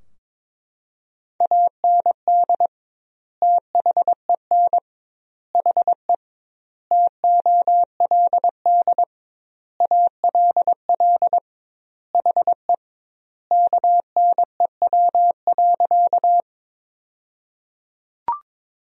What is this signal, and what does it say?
Morse code, heard as a keyed tone, about 21 wpm, 700 Hz. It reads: AND THEN HE TOLD ALL HE KNEW.
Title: 2x Word-Spacing - Sentences from Top 400 Words - 22wpm